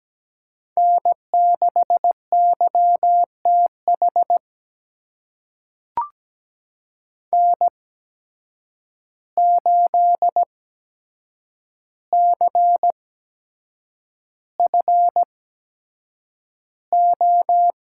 Morse code, heard as a keyed tone, about 17 wpm, spaced as for 7 wpm, 700 Hz. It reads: N6YTH N8CFO
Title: USA Call Signs - 5 Characters - 8x Character Spacing - 17wpm